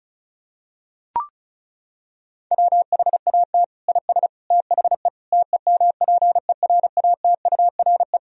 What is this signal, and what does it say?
Morse code, heard as a keyed tone, about 36 wpm, 700 Hz. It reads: WHAT IS THE TEMPERATURE
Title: Sentences from Top 1000 Words - Review Part 4 - 35wpm